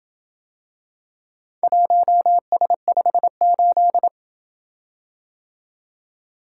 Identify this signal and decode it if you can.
Morse code, heard as a keyed tone, about 27 wpm, 700 Hz. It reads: 1S58